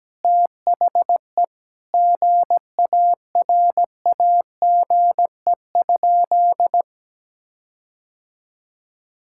Morse code, heard as a keyed tone, about 17 wpm, 700 Hz. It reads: THE GARAGE?
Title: Sentences from Top 1600 Words 17wpm